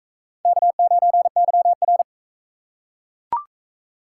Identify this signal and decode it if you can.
Morse code, heard as a keyed tone, about 39 wpm, 700 Hz. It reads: K9YR